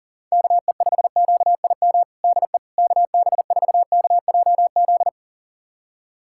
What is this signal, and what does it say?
Morse code, heard as a keyed tone, about 37 wpm, 700 Hz. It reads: KE5QIM DE KB4KJZ
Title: USA Callsign DE Callsign - 6 Characters - 40wpm